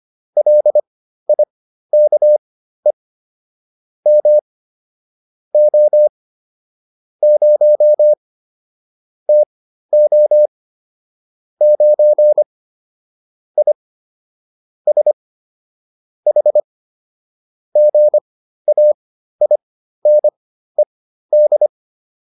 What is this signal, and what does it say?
Morse code, heard as a keyed tone, about 25 wpm, spaced as for 13 wpm, 600 Hz. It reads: LIKE M O 0 TO 9 I S H GAINED